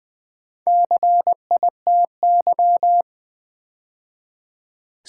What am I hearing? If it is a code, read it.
Morse code, heard as a keyed tone, about 20 wpm, 700 Hz. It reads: CITY
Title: Build A Word - 2 Letters At A Time - 4-Letter Words - 20wpm